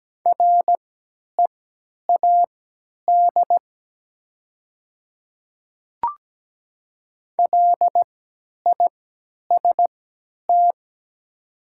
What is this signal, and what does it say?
Morse code, heard as a keyed tone, about 17 wpm, spaced as for 10 wpm, 700 Hz. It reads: READ LIST
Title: 3x Character Spacing - Top 500 Words - 17wpm